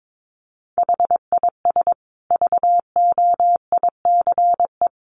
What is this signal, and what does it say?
Morse code, heard as a keyed tone, about 22 wpm, 700 Hz. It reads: HIS VOICE
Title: Sentences from Top 1100 Words 22wpm